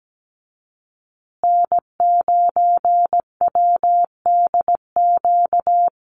Morse code, heard as a keyed tone, about 17 wpm, 700 Hz. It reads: N9WDQ
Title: USA Call Signs Repeated 2x 17wpm